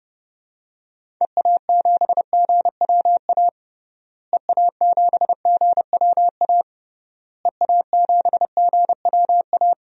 Morse code, heard as a keyed tone, about 30 wpm, 700 Hz. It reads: EA7GWA EA7GWA EA7GWA
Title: DXCC-100 Call Signs - 6 Characters - Repeated 3x - 30wpm